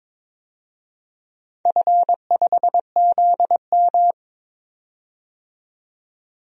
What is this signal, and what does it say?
Morse code, heard as a keyed tone, about 22 wpm, 700 Hz. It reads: F5ZM
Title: DXCC-100 Call Signs - 4 Characters - 22wpm